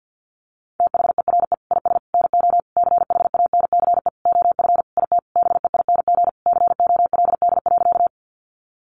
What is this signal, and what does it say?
Morse code, heard as a keyed tone, about 50 wpm, 700 Hz. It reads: THERE IS NO CHANCE OF IT BEING COLD.
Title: Sentences from Top 800 Words 50wpm